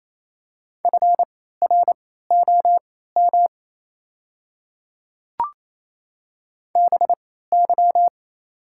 Morse code, heard as a keyed tone, about 27 wpm, spaced as for 16 wpm, 690 Hz. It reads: FROM BY
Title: 3x Character Spacing - Top 100 Words - 28wpm